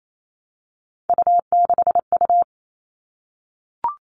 Morse code, heard as a keyed tone, about 28 wpm, 700 Hz. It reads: U6U
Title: All Cut Numbers - 3-Digit 28wpm